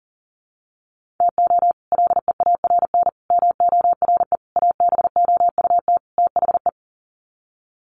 Morse code, heard as a keyed tone, about 40 wpm, 700 Hz. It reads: TO LEARN MORE ABOUT THE
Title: Sets of 5 Words Encore2 40wpm